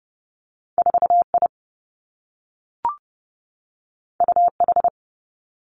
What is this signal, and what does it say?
Morse code, heard as a keyed tone, about 30 wpm, 700 Hz. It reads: 4 I U H
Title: Two Challenging Letter-Number Combinations 30wpm